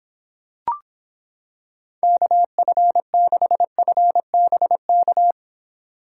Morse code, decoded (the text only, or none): KF6FBK